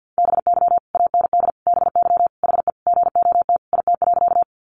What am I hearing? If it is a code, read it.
Morse code, heard as a keyed tone, about 50 wpm, 700 Hz. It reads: BY AND BY HE GOT IT.
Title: Sentences from Top 300 Words 50wpm